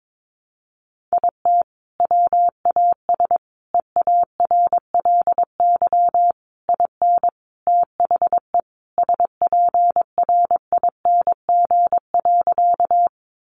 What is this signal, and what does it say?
Morse code, heard as a keyed tone, about 22 wpm, 700 Hz. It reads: IT WAS EARLY IN THE SPRING.
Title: Sentences from Top 1200 Words 22wpm